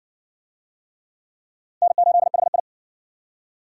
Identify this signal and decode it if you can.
Morse code, heard as a keyed tone, about 58 wpm, 700 Hz. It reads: N8HI